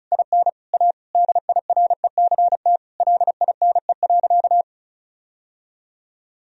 Morse code, heard as a keyed tone, about 36 wpm, 700 Hz. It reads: IN A DIRECT LINE.
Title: Sentences from Top 1000 Words - Review Part 2 - 35wpm